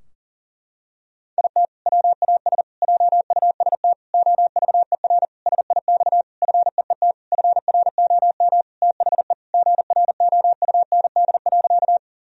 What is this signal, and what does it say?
Morse code, heard as a keyed tone, about 40 wpm, 700 Hz. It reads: IT WAS JUST OVER SIX FEET FROM THE GROUND.
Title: Sentences from Top 1000 Words - Review Part 2 - 40wpm